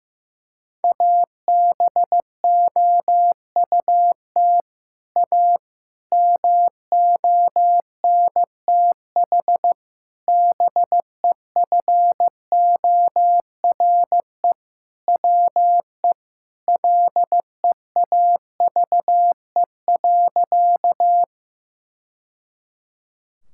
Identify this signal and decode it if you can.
Morse code, heard as a keyed tone, about 16 wpm, 700 Hz. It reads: ABOUT A MONTH BEFORE WE LEAVE.
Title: Sentences from Top 1000 Words - Review Part 5 - 15wpm